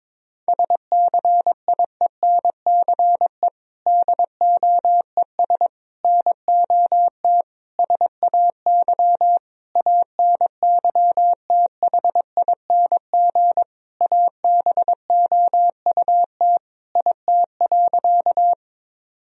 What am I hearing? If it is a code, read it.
Morse code, heard as a keyed tone, about 22 wpm, 700 Hz. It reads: SCIENCE DOES NOT SAY ANYTHING ABOUT IT.
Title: Sentences from Top 1000 Words - Review Part 2 - 22wpm